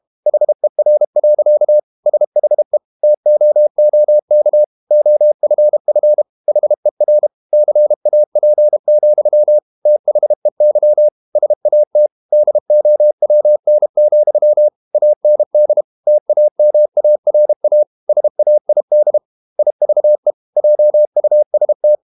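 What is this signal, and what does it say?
Morse code, heard as a keyed tone, about 32 wpm, 600 Hz. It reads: HER. SHE TOOK OFF HER CAP, THEY SAT DOWN, AND TAMARA SAID IVE JUST
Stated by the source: Morse code practice at 32 wpm from CW med Gnister, reading Ken Follett's novel Never in English.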